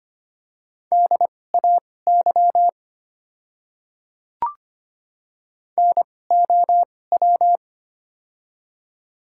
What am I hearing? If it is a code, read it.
Morse code, heard as a keyed tone, about 25 wpm, spaced as for 18 wpm, 700 Hz. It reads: DAY NOW
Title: 2x Character Spacing - Top 500 Words - 25wpm